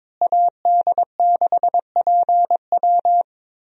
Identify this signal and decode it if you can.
Morse code, heard as a keyed tone, about 22 wpm, 700 Hz. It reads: AD6PW
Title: USA Call Signs - 5 Characters - Repeated 2x - 22wpm